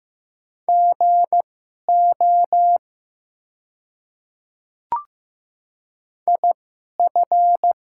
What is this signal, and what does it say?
Morse code, heard as a keyed tone, about 15 wpm, spaced as for 11 wpm, 700 Hz. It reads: GO IF
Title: 2x Character Spacing - Top 100 Words - 15wpm